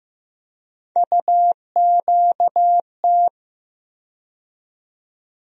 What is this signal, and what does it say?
Morse code, heard as a keyed tone, about 15 wpm, 700 Hz. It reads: UQT
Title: USA Call Sign Postfixes - 3 Letters 15wpm